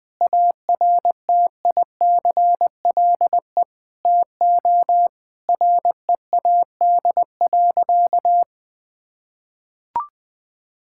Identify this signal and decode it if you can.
Morse code, heard as a keyed tone, about 20 wpm, 700 Hz. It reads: ARTICLE TO READ.
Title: Sentences from Top 1000 Words - Review Part 5 - 20wpm